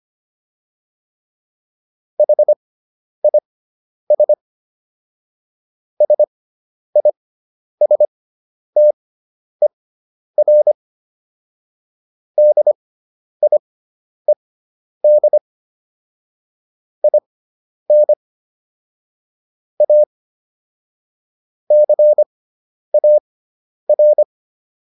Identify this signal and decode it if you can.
Morse code, heard as a keyed tone, about 25 wpm, spaced as for 10 wpm, 600 Hz. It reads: HIS SISTER DIED IN A CAR